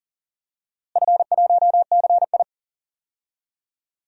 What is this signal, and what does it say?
Morse code, heard as a keyed tone, about 40 wpm, 700 Hz. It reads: F1CI
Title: DXCC-100 Call Signs - 4 Characters - Repeated 3x - 40wpm